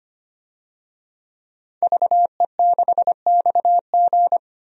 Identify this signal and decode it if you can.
Morse code, heard as a keyed tone, about 25 wpm, 700 Hz. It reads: VE6XG
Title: DXCC-100 Call Signs - 5 Characters - Repeated 2x - 25wpm